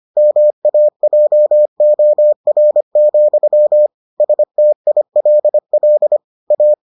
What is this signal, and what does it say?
Morse code, heard as a keyed tone, about 25 wpm, 600 Hz. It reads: MAJOR, STILL A